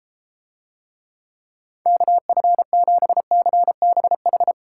Morse code, heard as a keyed tone, about 33 wpm, 700 Hz. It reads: KF7CBH